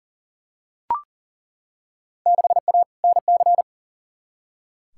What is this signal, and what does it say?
Morse code, heard as a keyed tone, about 40 wpm, 700 Hz. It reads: 6A NC
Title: Field Day Exchanges Encore - 40wpm